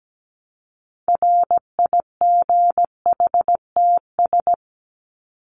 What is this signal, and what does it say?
Morse code, heard as a keyed tone, about 17 wpm, 700 Hz. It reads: RIGHTS
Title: ES:S Suffix - Top 100 Words with Base Words 17wpm